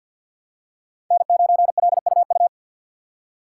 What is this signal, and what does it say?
Morse code, heard as a keyed tone, about 53 wpm, 700 Hz. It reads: N9LUU
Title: USA Call Signs - 5 Characters - 4x Character Spacing - 50wpm